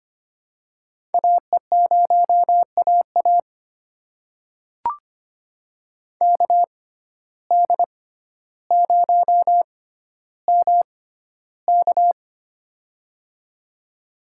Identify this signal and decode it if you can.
Morse code, heard as a keyed tone, about 25 wpm, spaced as for 13 wpm, 700 Hz. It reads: AE0AA KD0MK
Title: USA Call Signs - 5 Characters - 6x Character Spacing - 25wpm